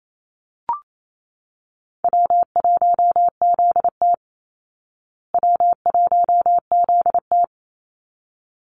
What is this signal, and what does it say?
Morse code, heard as a keyed tone, about 28 wpm, 700 Hz. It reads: W1ZT W1ZT
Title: Easy Call Signs Repeated 2x - 28wpm